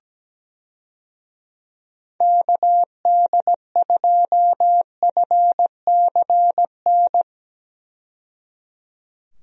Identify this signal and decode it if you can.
Morse code, heard as a keyed tone, about 17 wpm, 700 Hz. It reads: KD2FCN
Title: USA Call Signs - 6 Characters - 17wpm